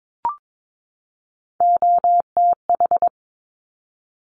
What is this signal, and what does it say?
Morse code, heard as a keyed tone, about 22 wpm, 700 Hz. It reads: OTH
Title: Top Three Letter Combinations in Words 22wpm